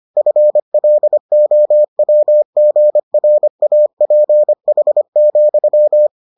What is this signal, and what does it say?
Morse code, heard as a keyed tone, about 25 wpm, 600 Hz. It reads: FLOWGRAPH,